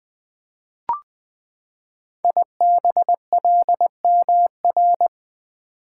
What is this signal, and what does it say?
Morse code, heard as a keyed tone, about 20 wpm, 700 Hz. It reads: IBLMR
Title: Code Groups - 5 Letters - 20wpm